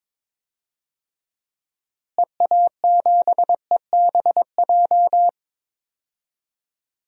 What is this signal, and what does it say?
Morse code, heard as a keyed tone, about 22 wpm, 700 Hz. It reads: EA7EBJ
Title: DXCC-100 Call Signs - 6 Characters - Repeated 2x - 22wpm